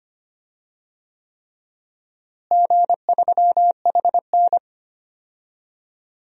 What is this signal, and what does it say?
Morse code, heard as a keyed tone, about 25 wpm, 700 Hz. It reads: G3HN
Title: DXCC-100 Call Signs - 4 Characters - Repeated 3x - 25wpm